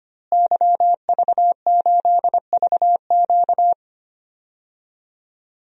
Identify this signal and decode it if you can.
Morse code, heard as a keyed tone, about 25 wpm, 700 Hz. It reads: YV8VQ
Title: DXCC-100 Call Signs - 5 Characters - Repeated 2x - 25wpm